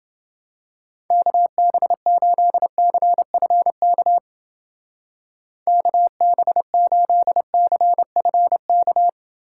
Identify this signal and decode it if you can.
Morse code, heard as a keyed tone, about 29 wpm, 700 Hz. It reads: KB8CFK KB8CFK